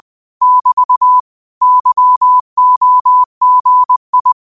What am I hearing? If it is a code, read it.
Morse code, heard as a keyed tone, about 20 wpm, 1000 Hz. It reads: <BT> YOGI
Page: Quotes for Thu, 14 Aug 2025 in Morse Code at 20 words per minute.